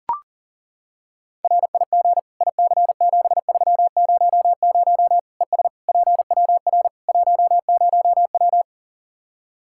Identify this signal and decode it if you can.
Morse code, heard as a keyed tone, about 40 wpm, 700 Hz. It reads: RIG IC7300 ES PWR 10W
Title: K1USN SST QSO Practice 40wpm